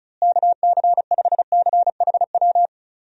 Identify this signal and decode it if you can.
Morse code, heard as a keyed tone, about 34 wpm, 700 Hz. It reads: KC5CHW
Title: USA Call Signs - 6 Characters - Repeated 3x - 35wpm